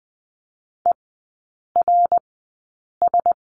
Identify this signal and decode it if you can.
Morse code, heard as a keyed tone, about 20 wpm, spaced as for 8 wpm, 700 Hz. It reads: ERS
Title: 093 Lesson 16e - Letter c - Three Characters 20wpm